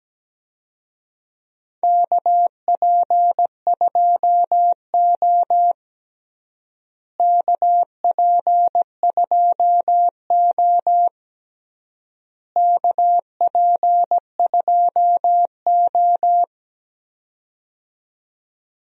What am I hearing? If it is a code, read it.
Morse code, heard as a keyed tone, about 17 wpm, 700 Hz. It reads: KP2O KP2O KP2O
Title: DXCC-100 Call Signs - 4 Characters - Repeated 3x - 17wpm